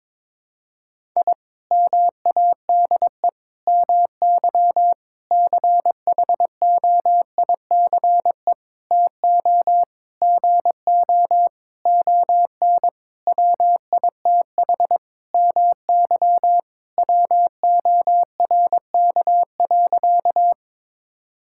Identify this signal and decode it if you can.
Morse code, heard as a keyed tone, about 21 wpm, 700 Hz. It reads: I MADE MY CHOICE TO GO ON WITH MY WORK.
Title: Sentences from Top 1000 Words - Review Part 6 - 22wpm